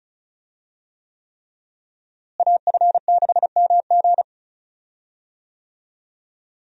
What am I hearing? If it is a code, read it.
Morse code, heard as a keyed tone, about 36 wpm, 700 Hz. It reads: AF6MG
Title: USA Call Signs - 5 Characters - Repeated 3x - 35wpm